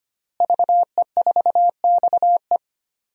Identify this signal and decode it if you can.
Morse code, heard as a keyed tone, about 25 wpm, 700 Hz. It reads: VE4XE
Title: DXCC-100 Call Signs - 5 Characters - Repeated 2x - 25wpm